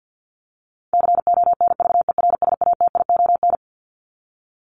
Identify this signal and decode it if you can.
Morse code, heard as a keyed tone, about 50 wpm, 700 Hz. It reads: CONVERSATION
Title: Top 1751-2000 Words 50wpm